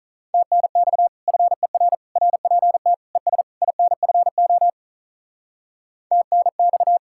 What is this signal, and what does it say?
Morse code, heard as a keyed tone, about 43 wpm, 700 Hz. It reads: TNX FER RPT ES INFO TNX
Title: Speed-Racing Common QSO Phrases - 35wpm